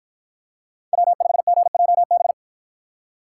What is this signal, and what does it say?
Morse code, heard as a keyed tone, about 52 wpm, 700 Hz. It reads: W5GJB